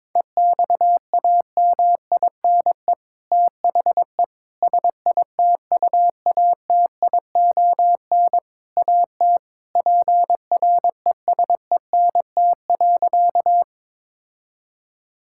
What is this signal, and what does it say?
Morse code, heard as a keyed tone, about 22 wpm, 700 Hz. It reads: EXAMINE THE SITUATION AT PRESENT.
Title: Sentences from Top 1600 Words 22wpm